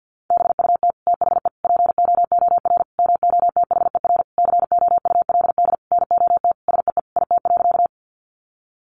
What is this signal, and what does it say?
Morse code, heard as a keyed tone, about 50 wpm, 700 Hz. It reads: BUT THE POOR MOTHER COULD NOT SEE IT.
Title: Sentences from Top 1000 Words 50wpm